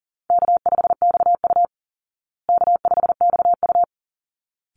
Morse code, heard as a keyed tone, about 40 wpm, 700 Hz. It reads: K5XU K5XU
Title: Call Signs Repeated 3x - 40wpm